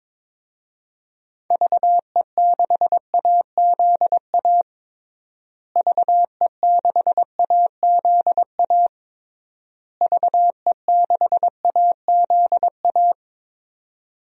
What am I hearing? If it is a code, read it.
Morse code, heard as a keyed tone, about 22 wpm, 700 Hz. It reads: VE6AZA VE6AZA VE6AZA
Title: DXCC-100 Call Signs - 6 Characters - Repeated 3x - 22wpm